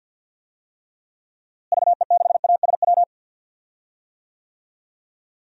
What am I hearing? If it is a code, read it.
Morse code, heard as a keyed tone, about 47 wpm, 700 Hz. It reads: VE6ASW
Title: DXCC-100 Call Signs - 6 Characters - Repeated 2x - 50wpm